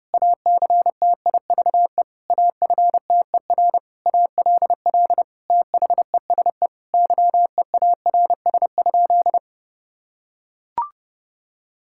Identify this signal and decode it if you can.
Morse code, heard as a keyed tone, about 32 wpm, 700 Hz. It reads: ACTIVE AFTER ALL THESE YEARS?
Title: Sentences from Top 1000 Words - Review Part 6 - 30wpm